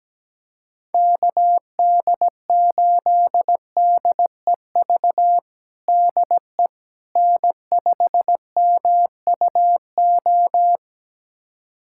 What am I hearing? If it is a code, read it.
Morse code, heard as a keyed tone, about 17 wpm, 700 Hz. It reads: KD8DEV DE N5MUO
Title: USA Callsign DE Callsign 17wpm